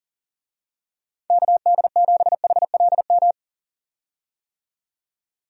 Morse code, heard as a keyed tone, about 40 wpm, 700 Hz. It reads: KD7HLM
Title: USA Call Signs - 6 Characters - 40wpm